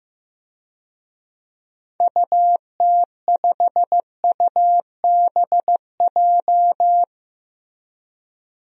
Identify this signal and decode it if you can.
Morse code, heard as a keyed tone, about 15 wpm, 700 Hz. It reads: UT5UBJ
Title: DXCC-100 Call Signs - 6 Characters - Repeated 3x - 15wpm